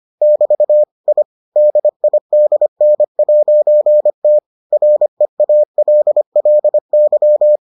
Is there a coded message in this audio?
Yes: Morse code from 0.2 to 7.6 s, heard as a keyed tone, about 25 wpm, 600 Hz. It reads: <BT> I DIDN'T REALLY